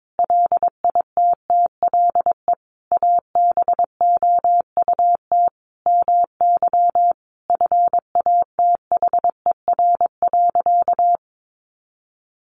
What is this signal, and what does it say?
Morse code, heard as a keyed tone, about 22 wpm, 700 Hz. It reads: LITTLE ABOUT MY FATHER.
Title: Sentences from Top 400 Words 22wpm